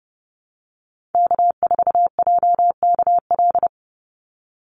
Morse code, heard as a keyed tone, about 28 wpm, 700 Hz. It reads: K4JKL
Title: Call Signs Repeated 2x - 30wpm